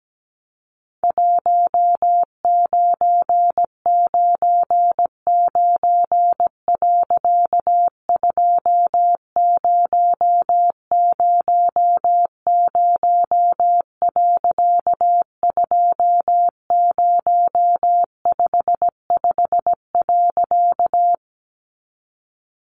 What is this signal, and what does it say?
Morse code, heard as a keyed tone, about 17 wpm, 700 Hz. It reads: 1999. 2000. 2055.
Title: A Sound Of Thunder 17wpm